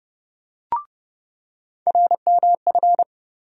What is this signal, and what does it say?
Morse code, heard as a keyed tone, about 32 wpm, 700 Hz. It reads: RMF
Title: USA Call Sign Postfixes - 3 Letters 30wpm